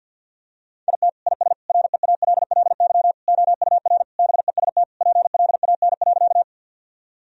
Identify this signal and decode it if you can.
Morse code, heard as a keyed tone, about 50 wpm, 700 Hz. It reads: IT IS REALLY OUR BEST PLAN.
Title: Sentences from Top 1000 Words - Review Part 2 - 50wpm